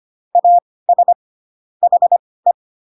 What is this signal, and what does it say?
Morse code, heard as a keyed tone, about 25 wpm, spaced as for 18 wpm, 700 Hz. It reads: AS HE